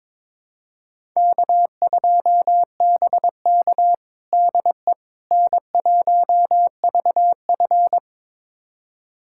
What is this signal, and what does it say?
Morse code, heard as a keyed tone, about 22 wpm, 700 Hz. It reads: K2BK DE N1VF
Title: USA Callsign DE Callsign - 4 Characters - 22wpm